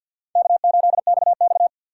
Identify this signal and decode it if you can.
Morse code, heard as a keyed tone, about 52 wpm, 700 Hz. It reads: K8XX
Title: USA Call Signs - 4 Characters - 3x Character Spacing - 50wpm